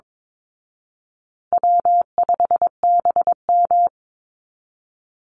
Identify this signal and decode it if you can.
Morse code, heard as a keyed tone, about 22 wpm, 700 Hz. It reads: W5BM
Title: Easy Call Signs Repeated 3x - 22wpm